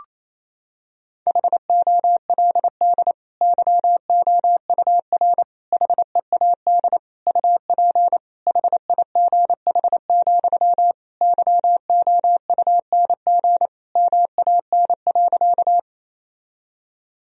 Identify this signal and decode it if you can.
Morse code, heard as a keyed tone, about 27 wpm, 700 Hz. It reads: HOLD YOUR HEAD UP HIGH, YOUNG MAN.
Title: Sentences from Top 1000 Words - Review Part 2 - 28wpm